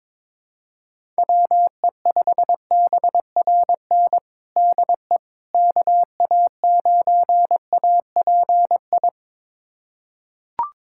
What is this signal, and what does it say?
Morse code, heard as a keyed tone, about 22 wpm, 700 Hz. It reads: WE5BRN DE KA9API
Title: USA Callsign DE Callsign - 6 Characters - 22wpm